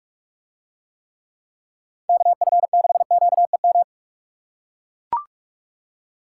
Morse code, heard as a keyed tone, about 44 wpm, 700 Hz. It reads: KF6QEM